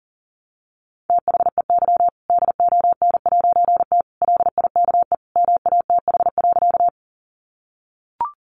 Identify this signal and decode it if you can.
Morse code, heard as a keyed tone, about 37 wpm, 700 Hz. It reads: THEY DON'T LIKE MATH.
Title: Easy Phrases 40wpm